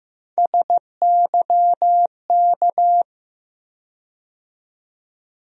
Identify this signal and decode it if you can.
Morse code, heard as a keyed tone, about 15 wpm, 700 Hz. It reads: SYK
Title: USA Call Sign Postfixes - 3 Letters 15wpm